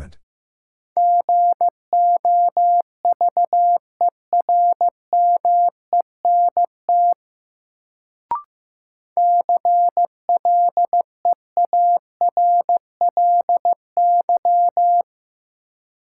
Morse code, heard as a keyed tone, about 16 wpm, 700 Hz. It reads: GOVERMENT CLEARLY
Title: Sentences from Top 1300 Words - New Word Drill - 15wpm